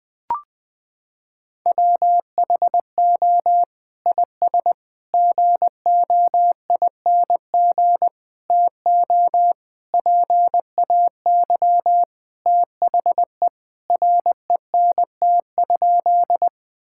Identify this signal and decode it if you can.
Morse code, heard as a keyed tone, about 20 wpm, 700 Hz. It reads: WHO IS GOING TO PAY THE RENT?
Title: Sentences from Top 1000 Words - Review Part 5 - 20wpm